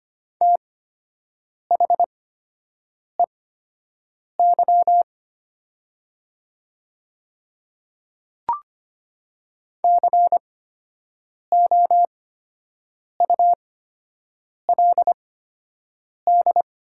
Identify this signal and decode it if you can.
Morse code, heard as a keyed tone, about 25 wpm, spaced as for 7 wpm, 700 Hz. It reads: THEY COULD
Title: 8x Character Spacing - Top 100 Words - 25wpm